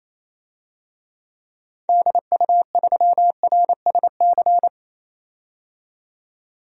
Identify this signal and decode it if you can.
Morse code, heard as a keyed tone, about 28 wpm, 700 Hz. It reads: DU3RSC
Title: DXCC-100 Call Signs - 6 Characters - 28wpm